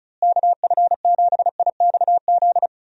Morse code, heard as a keyed tone, about 36 wpm, 700 Hz. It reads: KF7IXZ
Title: USA Call Signs - 6 Characters - Repeated 2x - 35wpm